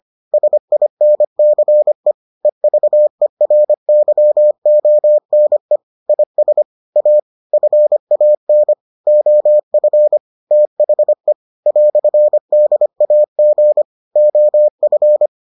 Morse code, heard as a keyed tone, about 25 wpm, 600 Hz. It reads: SINCE EVERYONE IS A FAN OF THE "DAG OF